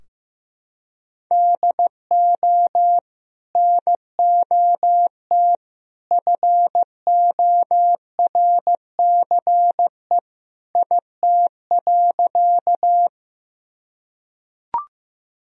Morse code, heard as a keyed tone, about 15 wpm, 700 Hz. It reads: DO NOT FORCE IT.
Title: Sentences from Top 1000 Words - Review Part 3 - 15wpm